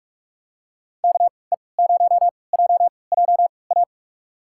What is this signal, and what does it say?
Morse code, heard as a keyed tone, about 41 wpm, spaced as for 25 wpm, 700 Hz. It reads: KE0JJA
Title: USA Call Signs - 6 Characters - 3x Character Spacing - 45wpm